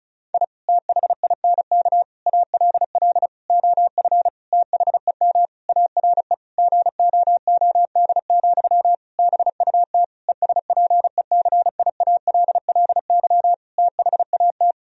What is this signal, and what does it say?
Morse code, heard as a keyed tone, about 34 wpm, 700 Hz. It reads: I THINK ALL OF THEM ARE GOOD, BUT ESPECIALLY THAT
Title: Sentences from Top 1000 Words - Review Part 4 - 35wpm